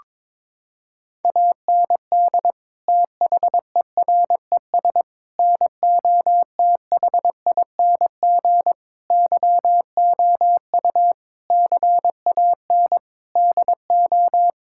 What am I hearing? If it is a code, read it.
Morse code, heard as a keyed tone, about 22 wpm, 700 Hz. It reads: AND THERES NOTHING YOU CAN DO
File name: practice / 22wpm / Sets of 8 Words - 22wpm